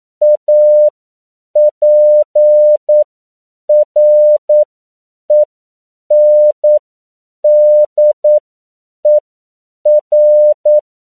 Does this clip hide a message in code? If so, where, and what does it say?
Yes: Morse code from 0.2 to 10.8 s, heard as a keyed tone, about 9 wpm, spaced as for 7 wpm, 600 Hz. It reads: APRENDER